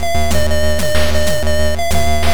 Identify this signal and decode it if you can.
There is no code.